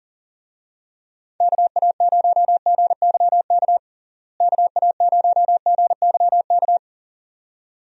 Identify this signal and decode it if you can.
Morse code, heard as a keyed tone, about 40 wpm, 700 Hz. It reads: KA0GYK KA0GYK
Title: USA Call Signs - 6 Characters - Repeated 2x - 40wpm